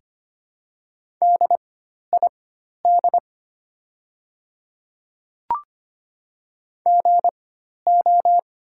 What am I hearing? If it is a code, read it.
Morse code, heard as a keyed tone, about 25 wpm, spaced as for 12 wpm, 690 Hz. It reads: DID GO
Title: 4x Character Spacing - Top 300 Words - 25wpm